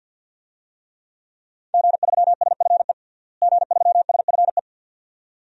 Morse code, heard as a keyed tone, about 53 wpm, 700 Hz. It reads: G3SFE G3SFE
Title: DXCC-100 Call Signs - 5 Characters - Repeated 2x - 50wpm